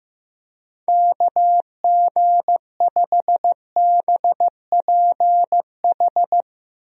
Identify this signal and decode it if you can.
Morse code, heard as a keyed tone, about 15 wpm, 700 Hz. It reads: KG5BPH
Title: USA Call Signs - 6 Characters - Repeated 3x - 15wpm